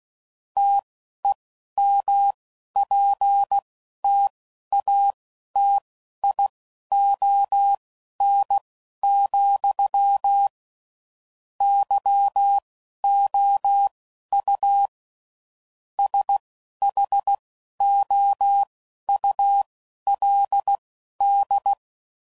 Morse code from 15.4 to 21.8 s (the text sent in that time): SHOULD